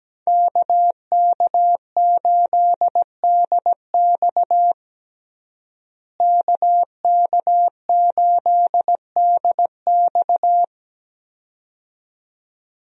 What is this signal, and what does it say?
Morse code, heard as a keyed tone, about 17 wpm, 700 Hz. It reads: KK8DX KK8DX
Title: USA Call Signs - 5 Characters - Repeated 2x - 17wpm